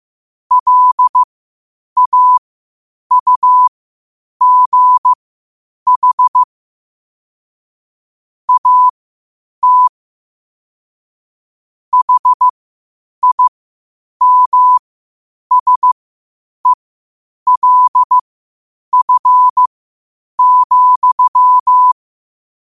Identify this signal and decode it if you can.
Morse code, heard as a keyed tone, about 15 wpm, spaced as for 8 wpm, 1000 Hz. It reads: LAUGH AT HIMSELF,
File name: qotdfiles / 8wpm